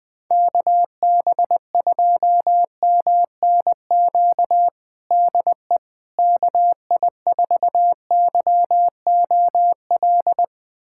Morse code, heard as a keyed tone, about 21 wpm, 700 Hz. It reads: KB2MNQ DE KI4YOL
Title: USA Callsign DE Callsign - 6 Characters - 20wpm